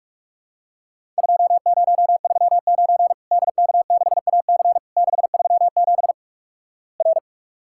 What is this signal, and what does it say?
Morse code, heard as a keyed tone, about 46 wpm, 700 Hz. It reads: 2039 DK6AC 637 R
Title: Worked All Europe DX Contest - QTC Exchange - 45wpm